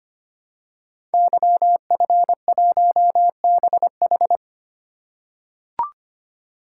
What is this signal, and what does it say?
Morse code, heard as a keyed tone, about 25 wpm, 700 Hz. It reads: YF1BH
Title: Code Groups - 5 Letter:Numbers - 25wpm